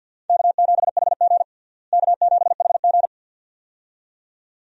Morse code, heard as a keyed tone, about 50 wpm, 700 Hz. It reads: K7HG K7HG
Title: USA Call Signs - 4 Characters - Repeated 2x - 50wpm